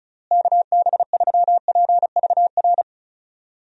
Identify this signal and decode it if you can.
Morse code, heard as a keyed tone, about 34 wpm, 700 Hz. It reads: KB3PVR
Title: USA Call Signs - 6 Characters - Repeated 2x - 35wpm